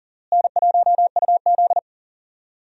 Morse code, heard as a keyed tone, about 40 wpm, 700 Hz. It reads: N1UZ
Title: USA Call Signs - 4 Characters - Repeated 2x - 40wpm